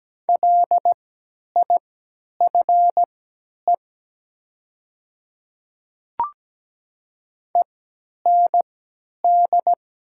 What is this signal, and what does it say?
Morse code, heard as a keyed tone, about 17 wpm, spaced as for 10 wpm, 700 Hz. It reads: LIFE END
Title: 3x Character Spacing - Top 300 Words - 17wpm